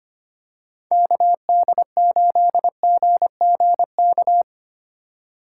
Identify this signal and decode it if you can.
Morse code, heard as a keyed tone, about 25 wpm, 700 Hz. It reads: KD8GGK
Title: USA Call Signs - 6 Characters - 2x Character Spacing - 25wpm